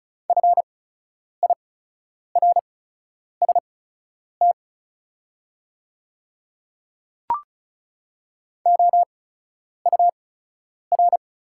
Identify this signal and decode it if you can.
Morse code, heard as a keyed tone, about 36 wpm, spaced as for 10 wpm, 690 Hz. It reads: FIRST OUR
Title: 8x Character Spacing - Top 200 Words - 35wpm